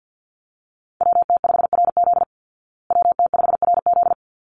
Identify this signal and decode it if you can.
Morse code, heard as a keyed tone, about 49 wpm, 700 Hz. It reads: WT5RZ WT5RZ
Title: Call Signs Repeated 3x - 50wpm